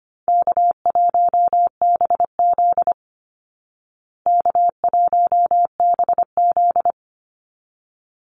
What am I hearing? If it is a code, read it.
Morse code, heard as a keyed tone, about 25 wpm, 700 Hz. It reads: K1BZ K1BZ
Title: Call Signs Repeated 3x - 25wpm